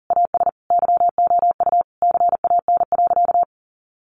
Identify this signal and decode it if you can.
Morse code, heard as a keyed tone, about 40 wpm, 700 Hz. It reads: AS YOU CAN.
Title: Sentences from Top 1200 Words 40wpm